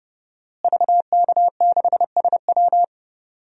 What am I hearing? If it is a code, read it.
Morse code, heard as a keyed tone, about 30 wpm, 700 Hz. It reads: VK6SW
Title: DXCC-100 Call Signs - 5 Characters - Repeated 2x - 30wpm